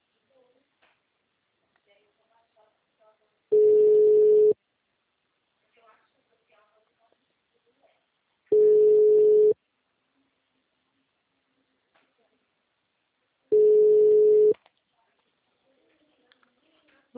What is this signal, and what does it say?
Ringback tone (Europe, 425 Hz), with speech.